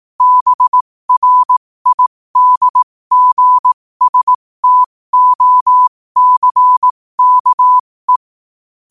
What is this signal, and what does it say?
Morse code, heard as a keyed tone, about 18 wpm, spaced as for 15 wpm, 1000 Hz. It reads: BRIDGSTOCKE
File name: qotdfiles / 15wpm